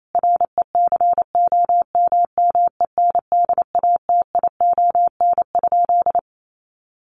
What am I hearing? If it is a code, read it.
Morse code, heard as a keyed tone, about 28 wpm, 700 Hz. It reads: RECOMMENDATION?
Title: Sentences from Top 1400 Words 28wpm